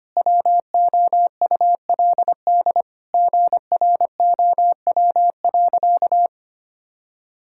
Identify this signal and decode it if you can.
Morse code, heard as a keyed tone, about 25 wpm, 700 Hz. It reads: WOULD GROW.
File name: practice / 25wpm / Sentences with 9-Word Phrases - 25wpm